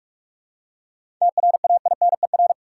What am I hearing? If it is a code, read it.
Morse code, heard as a keyed tone, about 47 wpm, 700 Hz. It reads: TRAINER
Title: Seven Letter Words - 45wpm